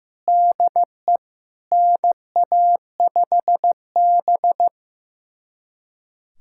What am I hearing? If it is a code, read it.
Morse code, heard as a keyed tone, about 15 wpm, 700 Hz. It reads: DE NA5B
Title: USA Callsign DE Callsign - 4 Characters - 15wpm